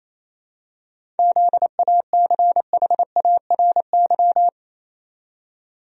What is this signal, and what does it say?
Morse code, heard as a keyed tone, about 27 wpm, 700 Hz. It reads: ZACHARY
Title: USA Common Names - 28wpm